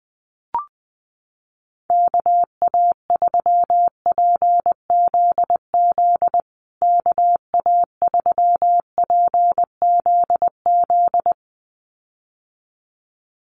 Morse code, heard as a keyed tone, about 20 wpm, 700 Hz. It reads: KA3PZZ KA3PZZ
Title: USA Call Signs Repeated 2x 20wpm